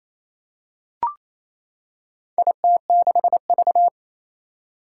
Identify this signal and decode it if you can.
Morse code, heard as a keyed tone, about 27 wpm, 700 Hz. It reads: IT6V
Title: Four Challenging Letter-Number Combinations 28wpm